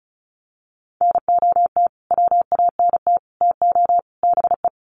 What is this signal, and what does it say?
Morse code, heard as a keyed tone, about 34 wpm, 700 Hz. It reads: NOT WANT TO BE
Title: Sets of 4 Words 35wpm